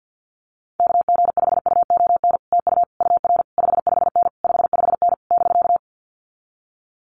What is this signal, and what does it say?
Morse code, heard as a keyed tone, about 50 wpm, 700 Hz. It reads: KG5UON TU UR 55N 55N <BK>
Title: Common QSO Phrases Encore 50wpm